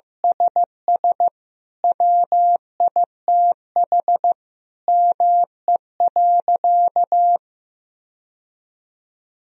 Morse code, heard as a keyed tone, about 16 wpm, 700 Hz. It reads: SS WITH ME.
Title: Sentences from Top 1000 Words - Review Part 1 - 15wpm